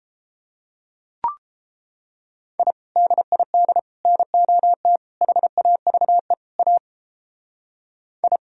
Morse code, heard as a keyed tone, about 29 wpm, 700 Hz. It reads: I DID NOT HAVE A I